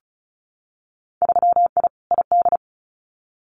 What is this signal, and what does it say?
Morse code, heard as a keyed tone, about 34 wpm, 700 Hz. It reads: 3I ID
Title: Winter Field Day Exchanges 35wpm